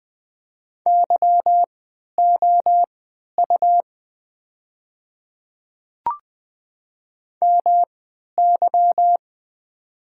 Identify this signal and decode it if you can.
Morse code, heard as a keyed tone, about 20 wpm, spaced as for 11 wpm, 700 Hz. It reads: YOU MY
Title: 3x Character Spacing - Top 300 Words - 20wpm